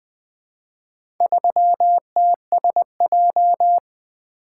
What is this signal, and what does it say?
Morse code, heard as a keyed tone, about 20 wpm, 700 Hz. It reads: 3TSJ
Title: Four Letter-Number Combinations - 20wpm